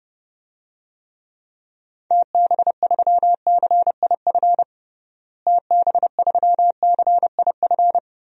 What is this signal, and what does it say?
Morse code, heard as a keyed tone, about 30 wpm, 700 Hz. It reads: TB3CIF TB3CIF
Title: DXCC-100 Call Signs - 6 Characters - Repeated 2x - 30wpm